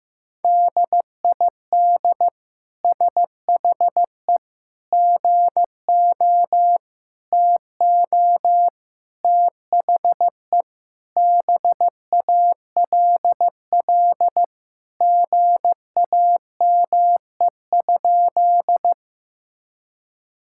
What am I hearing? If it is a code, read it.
Morse code, heard as a keyed tone, about 16 wpm, 700 Hz. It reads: DID SHE GO TO THE BALL GAME?
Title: Sentences from Top 1000 Words - Review Part 6 - 15wpm